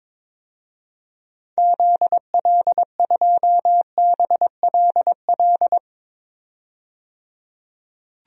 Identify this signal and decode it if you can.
Morse code, heard as a keyed tone, about 22 wpm, 700 Hz. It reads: ZL2BLL
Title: DXCC-100 Call Signs - 6 Characters - 22wpm